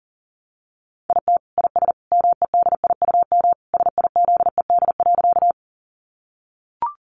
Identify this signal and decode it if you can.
Morse code, heard as a keyed tone, about 40 wpm, 700 Hz. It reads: IT IS MEDIUM SIZED.
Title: Easy Phrases 40wpm